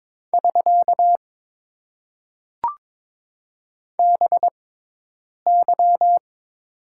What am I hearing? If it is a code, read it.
Morse code, heard as a keyed tone, about 22 wpm, spaced as for 8 wpm, 700 Hz. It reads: <SK> BY
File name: Top 90 QSO Elements - 6x Character Spacing - 22wpm